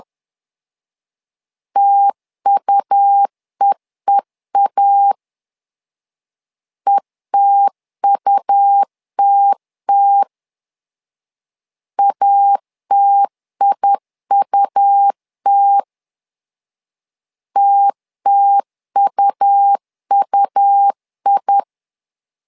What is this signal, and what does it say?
Morse code, heard as a keyed tone, about 10 wpm, 800 Hz. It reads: TUEEA ETUTT ATIUT TTUUI
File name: LEZIONI / 2 UHDN / 2 U 10wpm 1622